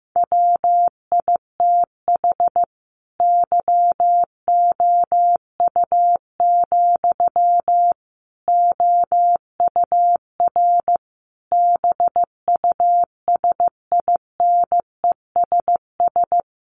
Morse code, heard as a keyed tone, about 15 wpm, 700 Hz. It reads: WITH YOU, OUR BUSINESS
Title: Sentences from Top 1400 Words 15wpm